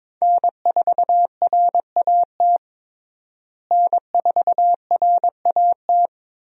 Morse code, heard as a keyed tone, about 22 wpm, 700 Hz. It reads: N4RAT N4RAT
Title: USA Call Signs - 5 Characters - Repeated 2x - 22wpm